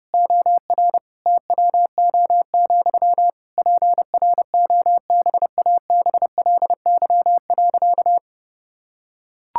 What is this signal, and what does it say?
Morse code, heard as a keyed tone, about 32 wpm, 700 Hz. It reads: OR TWO, PROBABLY.
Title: Sentences from Top 1000 Words - Review Part 5 - 30wpm